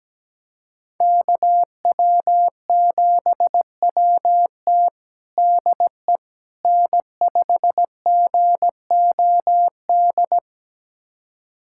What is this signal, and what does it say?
Morse code, heard as a keyed tone, about 17 wpm, 700 Hz. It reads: KW7WT DE N5GOD
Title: USA Callsign DE Callsign - 5 Characters - 17wpm